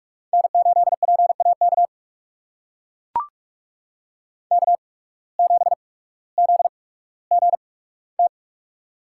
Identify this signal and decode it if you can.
Morse code, heard as a keyed tone, about 44 wpm, spaced as for 19 wpm, 700 Hz. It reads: N8PAK K7ZGT
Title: USA Call Signs - 5 Characters - 8x Character Spacing - 45wpm